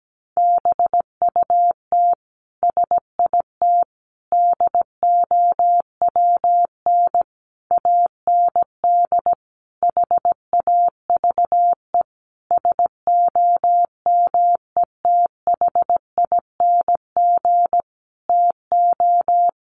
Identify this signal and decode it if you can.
Morse code, heard as a keyed tone, about 17 wpm, 700 Hz. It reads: BUT SIT DOWN AND HAVE SOMETHING TO
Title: Sentences from Top 600 Words 17wpm